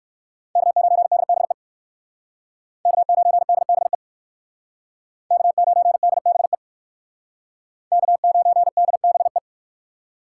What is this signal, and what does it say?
Morse code, heard as a keyed tone, about 59 wpm, 700 Hz. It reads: K9DBE K9DBE K9DBE K9DBE